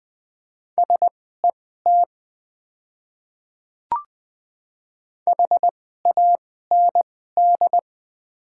Morse code, heard as a keyed tone, about 20 wpm, spaced as for 14 wpm, 700 Hz. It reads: SET HAND
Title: 2x Character Spacing - Top 200 Words - 20wpm